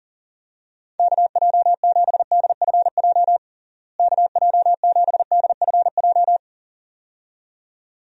Morse code, heard as a keyed tone, about 40 wpm, 700 Hz. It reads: KJ7DFJ KJ7DFJ
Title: USA Call Signs - 6 Characters - Repeated 2x - 40wpm